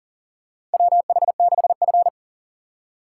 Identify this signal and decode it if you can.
Morse code, heard as a keyed tone, about 40 wpm, 700 Hz. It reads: WH6F
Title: USA Call Signs - 4 Characters - 2x Character Spacing - 40wpm